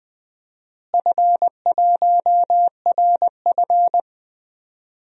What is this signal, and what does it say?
Morse code, heard as a keyed tone, about 20 wpm, 700 Hz. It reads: F1RF
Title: DXCC-100 Call Signs - 4 Characters - Repeated 2x - 20wpm